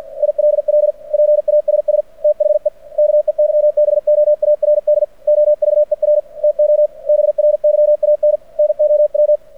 Morse code, heard as a keyed tone, about 53 wpm, 600 Hz. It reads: ILL PASS THE KEYBOARD OVER TO BRYAN NOW
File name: archived / CRSnet